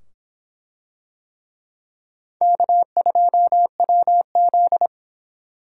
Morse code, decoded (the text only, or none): K2WZ